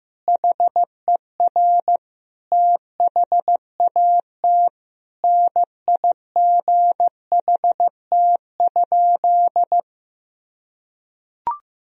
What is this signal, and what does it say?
Morse code, heard as a keyed tone, about 15 wpm, 700 Hz. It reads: HER THAT NIGHT?
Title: Sentences from Top 1000 Words - Review Part 2 - 15wpm